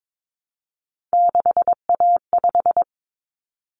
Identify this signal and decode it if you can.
Morse code, heard as a keyed tone, about 22 wpm, 700 Hz. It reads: 6A5
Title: Common Cut Numbers - 3-Digit 22wpm